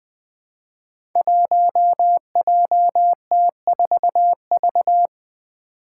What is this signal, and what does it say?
Morse code, heard as a keyed tone, about 20 wpm, 700 Hz. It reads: 1JT4V
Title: Five Challenging Letter-Number Combinations 20wpm